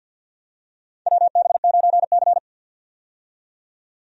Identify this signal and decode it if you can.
Morse code, heard as a keyed tone, about 51 wpm, 700 Hz. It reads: WB9C
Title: USA Call Signs - 4 Characters - 50wpm